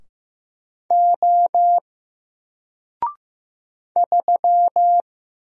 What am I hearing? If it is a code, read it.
Morse code, heard as a keyed tone, about 15 wpm, 700 Hz. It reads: O 3
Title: Single Character (Letter:Number:Punctuation:Prosign) Alt 15wpm